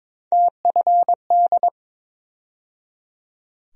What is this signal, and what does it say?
Morse code, heard as a keyed tone, about 22 wpm, 700 Hz. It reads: TFD
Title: USA Call Sign Postfixes - 3 Letters 22wpm